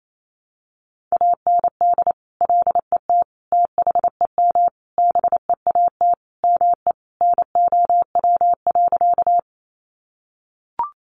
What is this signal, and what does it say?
Morse code, heard as a keyed tone, about 28 wpm, 700 Hz. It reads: AND LET THEM BEAT ME NOW.
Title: Sentences from Top 1300 Words 28wpm